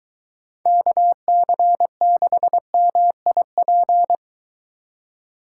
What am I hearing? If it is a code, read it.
Morse code, heard as a keyed tone, about 23 wpm, 700 Hz. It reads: KC6MIP